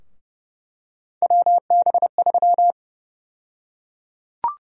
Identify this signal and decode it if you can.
Morse code, heard as a keyed tone, about 30 wpm, 700 Hz. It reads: WB3
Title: USA Call Sign Prefixes - Repeated 2x - 30wpm